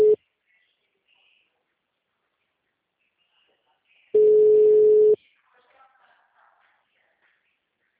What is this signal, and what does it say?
Ringback tone (Europe, 425 Hz), with speech.